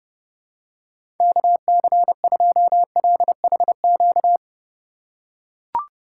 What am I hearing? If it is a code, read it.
Morse code, heard as a keyed tone, about 31 wpm, 700 Hz. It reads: KC2LHQ
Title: USA Call Signs - 6 Characters - 2x Character Spacing - 30wpm